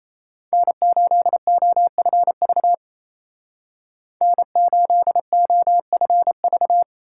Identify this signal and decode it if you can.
Morse code, heard as a keyed tone, about 36 wpm, 700 Hz. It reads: N8OFV N8OFV